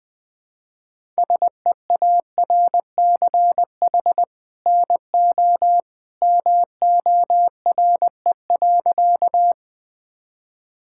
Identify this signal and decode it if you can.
Morse code, heard as a keyed tone, about 20 wpm, 700 Hz. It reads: SEARCH NO MORE.
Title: Sentences from Top 1000 Words - Review Part 5 - 20wpm